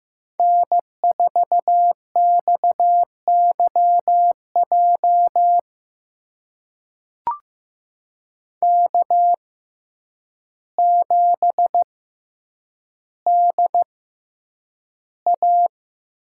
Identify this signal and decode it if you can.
Morse code, heard as a keyed tone, about 15 wpm, spaced as for 8 wpm, 700 Hz. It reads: N4XYJ K7DA
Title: USA Call Signs - 5 Characters - 6x Character Spacing - 15wpm